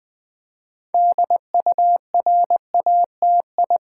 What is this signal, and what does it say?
Morse code, heard as a keyed tone, about 20 wpm, 700 Hz. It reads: DURATI
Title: Build A Word - 3 Letters At A Time - 8-Letter Words - 20wpm